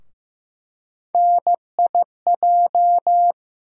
Morse code, heard as a keyed tone, about 15 wpm, 700 Hz. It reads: NIJ
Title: USA Call Sign Postfixes - 3 Letters 15wpm